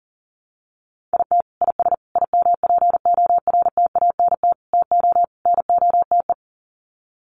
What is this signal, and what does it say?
Morse code, heard as a keyed tone, about 40 wpm, 700 Hz. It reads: IT IS IMPORTANT TO NOTE
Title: Sets of 5 Words 40wpm